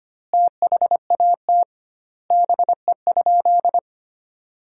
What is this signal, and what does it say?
Morse code, heard as a keyed tone, about 25 wpm, 700 Hz. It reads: THAT BE?
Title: 2x Word-Spacing - Sentences from Top 100 Words - 25wpm